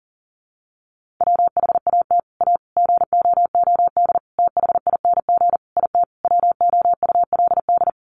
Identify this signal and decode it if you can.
Morse code, heard as a keyed tone, about 40 wpm, 700 Hz. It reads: WHAT A GOOD THING IT WOULD
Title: Sentences from Top 200 Words 40wpm